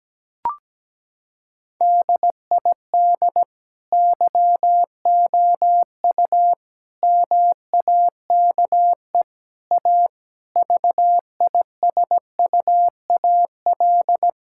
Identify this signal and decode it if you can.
Morse code, heard as a keyed tone, about 17 wpm, 700 Hz. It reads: DID YOU MAKE A VISUAL
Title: Sentences from Top 1000 Words - Review Part 6 - 17wpm